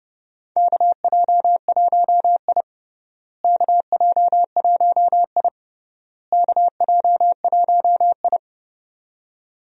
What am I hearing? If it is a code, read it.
Morse code, heard as a keyed tone, about 32 wpm, 700 Hz. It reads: KJ1I KJ1I KJ1I
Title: USA Call Signs - 4 Characters - Repeated 3x - 30wpm